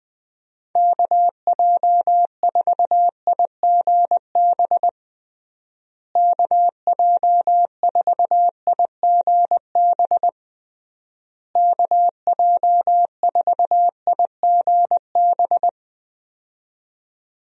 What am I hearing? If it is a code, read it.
Morse code, heard as a keyed tone, about 20 wpm, 700 Hz. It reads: KJ4IGB KJ4IGB KJ4IGB
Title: USA Call Signs - 6 Characters - Repeated 3x - 20wpm